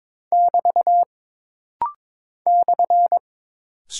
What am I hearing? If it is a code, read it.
Morse code, heard as a keyed tone, about 23 wpm, 700 Hz. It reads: <BT> /
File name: practice / 22wpm / Single Punctuation-Prosign - Rapid-Fire - 22wpm